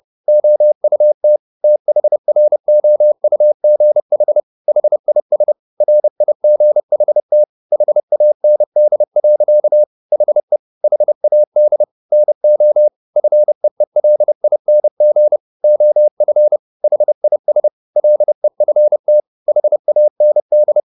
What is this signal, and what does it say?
Morse code, heard as a keyed tone, about 30 wpm, 600 Hz. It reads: OUT THROUGH HIS RIGHT HAND. HE HAD NO FEELING OF HIS LEFT HAND